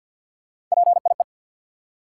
Morse code, heard as a keyed tone, about 51 wpm, 700 Hz. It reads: PIE
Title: Three Letter Words - 50wpm